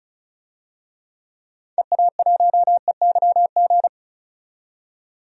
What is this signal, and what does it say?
Morse code, heard as a keyed tone, about 36 wpm, 700 Hz. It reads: EA1EYG
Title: DXCC-100 Call Signs - 6 Characters - 35wpm